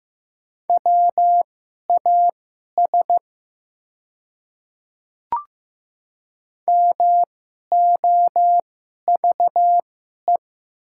Morse code, heard as a keyed tone, about 15 wpm, spaced as for 11 wpm, 700 Hz. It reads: WAS MOVE
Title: 2x Character Spacing - Top 300 Words - 15wpm